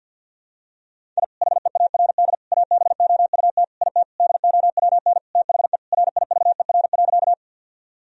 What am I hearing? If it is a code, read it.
Morse code, heard as a keyed tone, about 50 wpm, 700 Hz. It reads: I HEARD ABOUT IT DOWN THE RIVER.
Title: Sentences from Top 1000 Words - Review Part 1 - 50wpm